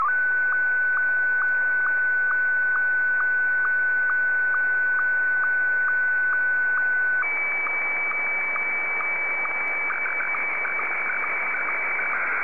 SSTV (slow-scan television), Martin 1.